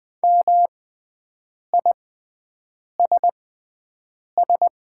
Morse code, heard as a keyed tone, about 20 wpm, spaced as for 7 wpm, 700 Hz. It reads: MISS
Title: 6x Character Spacing - Top 500 Words - 20wpm